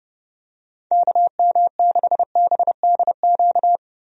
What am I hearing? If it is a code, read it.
Morse code, heard as a keyed tone, about 30 wpm, 700 Hz. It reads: KM6BDQ
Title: USA Call Signs - 6 Characters - 30wpm